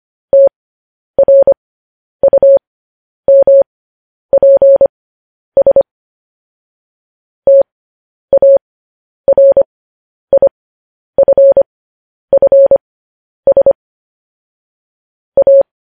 Morse code, heard as a keyed tone, about 25 wpm, spaced as for 10 wpm, 550 Hz. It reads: TRUMPS TARIFFS A
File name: podcast / News headlines-1x-25x10